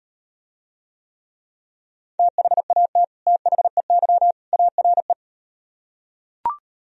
Morse code, heard as a keyed tone, about 38 wpm, 700 Hz. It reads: THAT THEY ARE